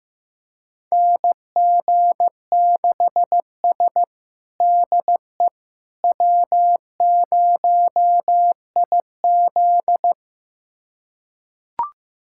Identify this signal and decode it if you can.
Morse code, heard as a keyed tone, about 15 wpm, 700 Hz. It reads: NG6S DE W0IZ
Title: USA Callsign DE Callsign - 4 Characters - 15wpm